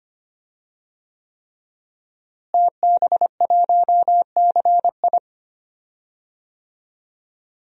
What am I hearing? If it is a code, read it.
Morse code, heard as a keyed tone, about 25 wpm, 700 Hz. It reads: TB1CI
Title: DXCC-100 Call Signs - 5 Characters - Repeated 3x - 25wpm